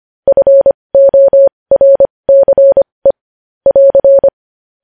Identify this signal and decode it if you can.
Morse code, heard as a keyed tone, about 25 wpm, spaced as for 20 wpm, 550 Hz. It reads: FORCE <AR>
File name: podcast / News headlines-1x-25x20